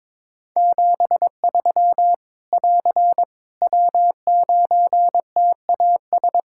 Morse code, heard as a keyed tone, about 22 wpm, 700 Hz. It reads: 73 <AR> W9TAS
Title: Practice QSO - Exchange 4 - Ending - 22wpm